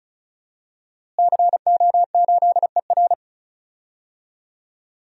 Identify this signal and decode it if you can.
Morse code, heard as a keyed tone, about 36 wpm, 700 Hz. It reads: CO8ER
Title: DXCC-100 Call Signs - 5 Characters - Repeated 2x - 35wpm